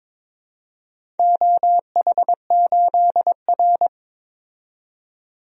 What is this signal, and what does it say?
Morse code, heard as a keyed tone, about 22 wpm, 700 Hz. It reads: OH8R
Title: DXCC-100 Call Signs - 4 Characters - 22wpm